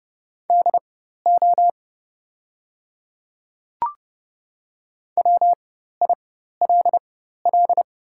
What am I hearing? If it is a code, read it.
Morse code, heard as a keyed tone, about 30 wpm, spaced as for 14 wpm, 700 Hz. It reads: DO WILL
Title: 4x Character Spacing - Top 200 Words - 30wpm